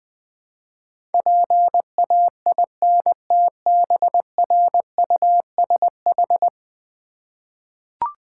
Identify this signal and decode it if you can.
Morse code, heard as a keyed tone, about 20 wpm, 700 Hz. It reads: PAINTBRUSH
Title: Easy Words with Hints - 20wpm